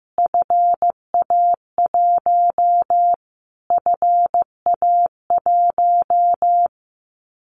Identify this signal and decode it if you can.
Morse code, heard as a keyed tone, about 15 wpm, 700 Hz. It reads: FA1 FA1
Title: Callsign Prefixes for Top 25 DXCC - Repeated 3x - 15wpm